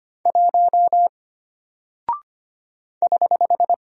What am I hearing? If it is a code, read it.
Morse code, heard as a keyed tone, about 25 wpm, 700 Hz. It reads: 1 <HH>
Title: Single Character (Letter:Number:Punctuation:Prosign) Alt 25wpm